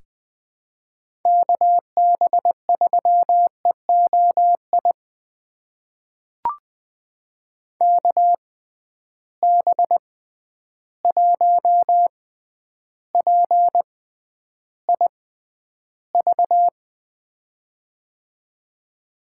Morse code, heard as a keyed tone, about 20 wpm, spaced as for 10 wpm, 700 Hz. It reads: KB3EOI KB1PIV
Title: USA Call Signs - 6 Characters - 6x Character Spacing - 20wpm